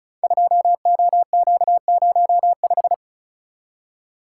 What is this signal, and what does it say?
Morse code, heard as a keyed tone, about 36 wpm, 690 Hz. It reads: 2OQ05
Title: Code Groups - 5 Letter:Numbers - 35wpm